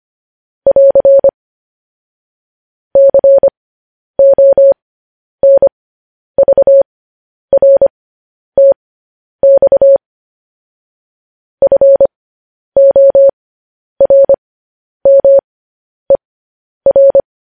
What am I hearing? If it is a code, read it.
Morse code, heard as a keyed tone, about 25 wpm, spaced as for 10 wpm, 550 Hz. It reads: <AR> CONVRTX FORMER
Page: news headlines in morse code at 10 wpm